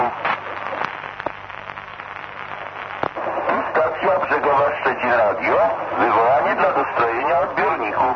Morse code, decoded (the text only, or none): none